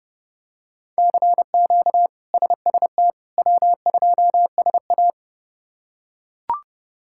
Morse code, heard as a keyed tone, about 30 wpm, 700 Hz. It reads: CQ SST W2SA
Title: K1USN SST QSO Practice - 30wpm